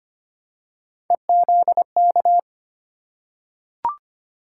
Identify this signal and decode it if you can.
Morse code, heard as a keyed tone, about 25 wpm, 700 Hz. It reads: EZK
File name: Three Letter-Number Combinations - 25wpm